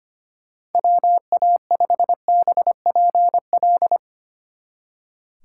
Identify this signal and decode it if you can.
Morse code, heard as a keyed tone, about 32 wpm, 700 Hz. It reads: WA5BPL